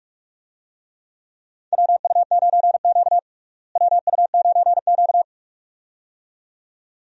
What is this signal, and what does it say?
Morse code, heard as a keyed tone, about 45 wpm, 700 Hz. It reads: WU9Q WU9Q
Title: USA Call Signs - 4 Characters - Repeated 2x - 45wpm